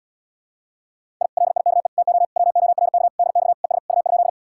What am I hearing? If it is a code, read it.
Morse code, heard as a keyed tone, about 80 wpm, 700 Hz. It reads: I LIKE TO WORK DX ES U?